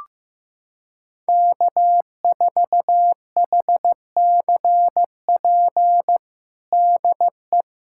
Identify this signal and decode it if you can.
Morse code, heard as a keyed tone, about 15 wpm, 700 Hz. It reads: K4HCP DE
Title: USA Callsign DE Callsign - 5 Characters - 15wpm